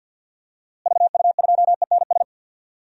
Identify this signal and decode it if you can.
Morse code, heard as a keyed tone, about 48 wpm, 700 Hz. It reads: VU2ENS
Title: DXCC-100 Call Signs - 6 Characters - Repeated 3x - 50wpm